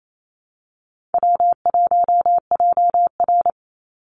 Morse code, heard as a keyed tone, about 28 wpm, 700 Hz. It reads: W1JR
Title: Call Signs Repeated 2x - 28wpm